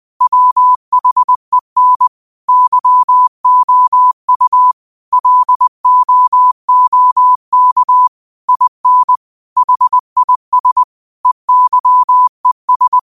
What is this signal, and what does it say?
Morse code, heard as a keyed tone, about 20 wpm, 1000 Hz. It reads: WHEN YOU LOOK IN HIS EYES
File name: qotdfiles / 20wpm